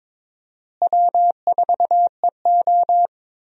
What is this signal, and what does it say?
Morse code, heard as a keyed tone, about 21 wpm, 700 Hz. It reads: W4EO
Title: USA Call Signs - 4 Characters - Repeated 3x - 22wpm